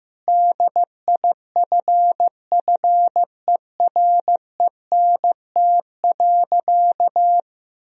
Morse code, heard as a keyed tone, about 15 wpm, 700 Hz. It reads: DIFFERENT.
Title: Sentences from Top 1000 Words - Review Part 4 - 15wpm